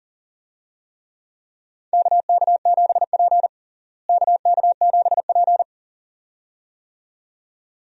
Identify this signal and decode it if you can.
Morse code, heard as a keyed tone, about 40 wpm, 700 Hz. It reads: KK7P KK7P
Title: USA Call Signs - 4 Characters - Repeated 2x - 40wpm